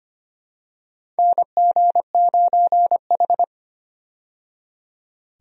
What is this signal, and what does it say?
Morse code, heard as a keyed tone, about 25 wpm, 700 Hz. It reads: NG9H
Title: USA Call Signs - 4 Characters - 25wpm